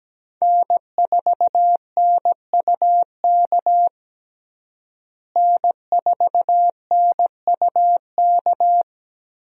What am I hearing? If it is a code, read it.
Morse code, heard as a keyed tone, about 17 wpm, 700 Hz. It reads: N4NUK N4NUK
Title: USA Call Signs - 5 Characters - Repeated 2x - 17wpm